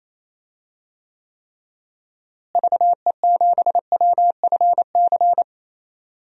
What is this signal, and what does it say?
Morse code, heard as a keyed tone, about 27 wpm, 700 Hz. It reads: VE7WFC
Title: DXCC-100 Call Signs - 6 Characters - Repeated 3x - 28wpm